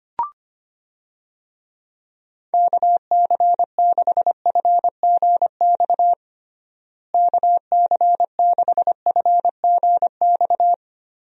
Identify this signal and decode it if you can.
Morse code, heard as a keyed tone, about 25 wpm, 700 Hz. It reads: KC6FGX KC6FGX
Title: USA Call Signs - 6 Characters - Repeated 3x - 25wpm